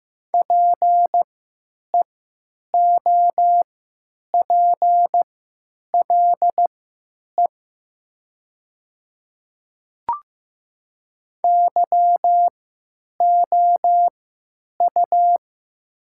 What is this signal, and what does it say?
Morse code, heard as a keyed tone, about 15 wpm, spaced as for 9 wpm, 700 Hz. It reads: PEOPLE YOU
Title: 3x Character Spacing - Top 200 Words - 15wpm